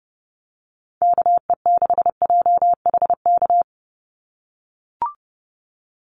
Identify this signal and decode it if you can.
Morse code, heard as a keyed tone, about 30 wpm, 700 Hz. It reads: KE6JHK
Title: USA Call Signs 30wpm